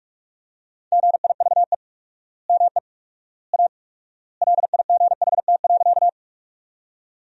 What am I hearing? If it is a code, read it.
Morse code, heard as a keyed tone, about 46 wpm, 700 Hz. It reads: GIVE ME A LIGHT.
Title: 4x Word-Spacing - Sentences from Top 200 Words - 45wpm